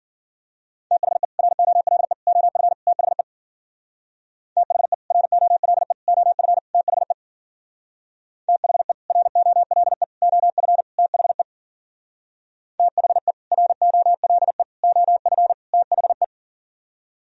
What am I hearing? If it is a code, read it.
Morse code, heard as a keyed tone, about 52 wpm, 700 Hz. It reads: THE ROLE OF THE THE ROLE OF THE THE ROLE OF THE THE ROLE OF THE